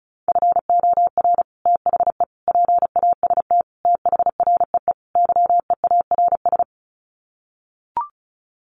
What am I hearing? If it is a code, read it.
Morse code, heard as a keyed tone, about 34 wpm, 700 Hz. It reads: FOR THE PAST THREE YEARS
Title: Sets of 5 Words Encore2 35wpm